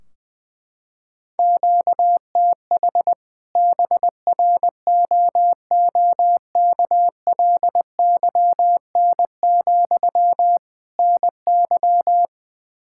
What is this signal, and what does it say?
Morse code, heard as a keyed tone, about 20 wpm, 700 Hz. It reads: QTH BROOKLYN, NY
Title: Practice QSO - Exchange 1 - Beginning - 20wpm